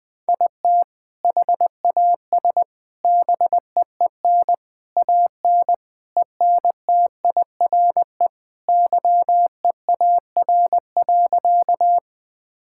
Morse code, heard as a keyed tone, about 21 wpm, 700 Hz. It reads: IT HAS BEEN AN ENTIRE YEAR.
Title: Sentences from Top 1000 Words - Review Part 5 - 20wpm